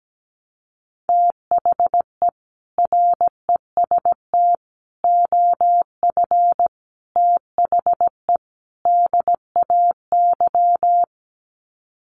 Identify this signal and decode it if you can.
Morse code, heard as a keyed tone, about 17 wpm, 700 Hz. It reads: THE REST OF THE DAY
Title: Sets of 5 Words 17wpm